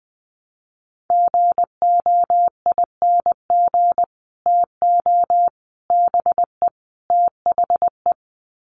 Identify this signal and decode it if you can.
Morse code, heard as a keyed tone, about 20 wpm, 700 Hz. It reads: GOING TO BE THE
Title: Sets of 4 Words 20wpm